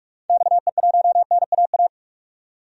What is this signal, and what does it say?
Morse code, heard as a keyed tone, about 47 wpm, 700 Hz. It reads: XE1NAA
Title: DXCC-100 Call Signs - 6 Characters - 45wpm